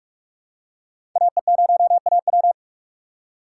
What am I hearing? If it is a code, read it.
Morse code, heard as a keyed tone, about 46 wpm, 700 Hz. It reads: AE0AW
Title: USA Call Signs - 5 Characters - 3x Character Spacing - 45wpm